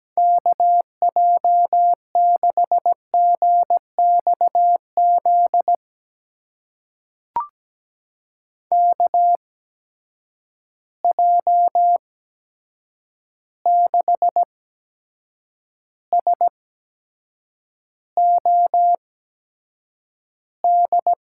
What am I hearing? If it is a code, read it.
Morse code, heard as a keyed tone, about 17 wpm, spaced as for 7 wpm, 700 Hz. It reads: KJ6GXZ KJ6SOD
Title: USA Call Signs - 6 Characters - 8x Character Spacing - 17wpm